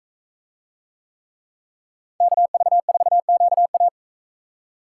Morse code, heard as a keyed tone, about 42 wpm, 700 Hz. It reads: KV4QA